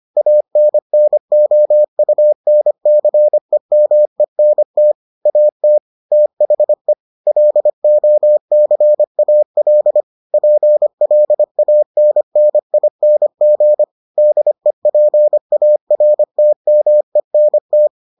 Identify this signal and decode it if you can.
Morse code, heard as a keyed tone, about 25 wpm, 600 Hz. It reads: ANNOUNCEMENT AT THE LOCAL PLANNING DEPARTMENT